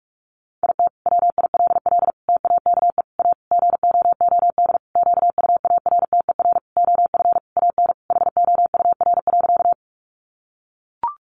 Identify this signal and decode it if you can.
Morse code, heard as a keyed tone, about 44 wpm, 700 Hz. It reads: IT WILL TAKE A GOOD QUARTER OF AN HOUR.
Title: Sentences from Top 1300 Words 45wpm